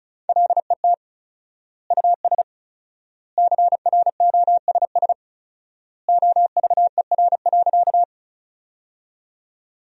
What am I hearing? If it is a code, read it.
Morse code, heard as a keyed tone, about 36 wpm, 700 Hz. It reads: LET US CROSS OVER.
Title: 4x Word-Spacing - Sentences from Top 300 Words - 35wpm